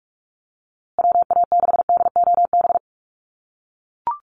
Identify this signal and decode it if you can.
Morse code, heard as a keyed tone, about 44 wpm, 700 Hz. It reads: WA6DOB
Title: USA Call Signs Repeated 2x 45wpm